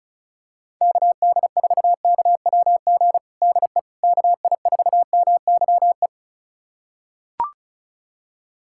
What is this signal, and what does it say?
Morse code, heard as a keyed tone, about 36 wpm, 700 Hz. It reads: KD4KWG DE KI4MYE
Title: USA Callsign DE Callsign 35wpm